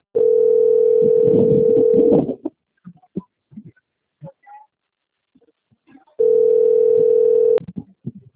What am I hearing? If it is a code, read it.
Ringback tone (North America), 440 + 480 Hz.